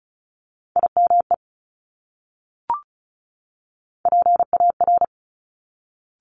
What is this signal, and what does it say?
Morse code, heard as a keyed tone, about 36 wpm, 700 Hz. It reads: IME PAR
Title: Top Three Letter Combinations in Words 35wpm